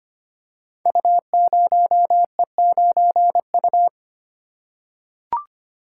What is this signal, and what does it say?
Morse code, heard as a keyed tone, about 25 wpm, 700 Hz. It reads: U0E9U
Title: Code Groups - 5 Letter:Numbers - 25wpm